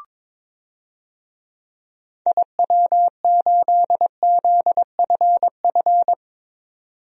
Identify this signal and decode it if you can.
Morse code, heard as a keyed tone, about 22 wpm, 700 Hz. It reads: IW8ZFF
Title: DXCC-100 Call Signs - 6 Characters - 22wpm